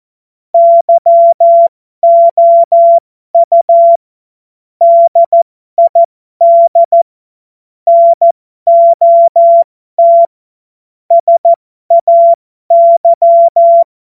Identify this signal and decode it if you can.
Morse code, heard as a keyed tone, about 14 wpm, spaced as for 12 wpm, 680 Hz. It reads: YOU DID NOT SAY
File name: Morsekurs / Witze